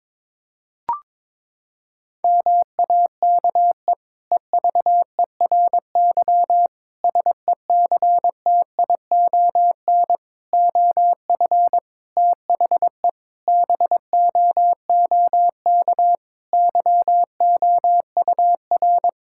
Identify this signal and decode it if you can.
Morse code, heard as a keyed tone, about 22 wpm, 700 Hz. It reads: MAKE EVERY SECTION OF THE BOOK YOUR
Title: Sentences from Top 1000 Words - Review Part 5 - 22wpm